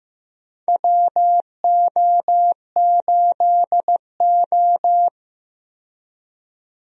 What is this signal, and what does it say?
Morse code, heard as a keyed tone, about 15 wpm, 700 Hz. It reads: WO8O
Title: USA Call Signs - 4 Characters - 15wpm